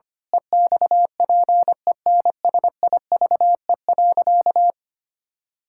Morse code, heard as a keyed tone, about 25 wpm, 700 Hz. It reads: EXPENSIVE.
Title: Pangram Challenge v2 - 25wpm